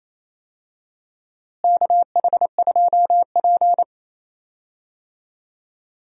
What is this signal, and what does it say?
Morse code, heard as a keyed tone, about 28 wpm, 700 Hz. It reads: KH2P
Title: USA Call Signs - 4 Characters - Repeated 2x - 28wpm